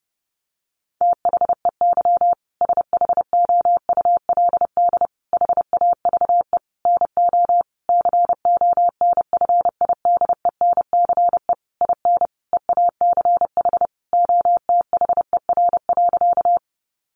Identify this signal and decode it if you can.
Morse code, heard as a keyed tone, about 30 wpm, 700 Hz. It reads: THEY SHOULD HAVE NO CONFIDENCE IN EACH OTHER.
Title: Sentences from Top 1500 Words 30wpm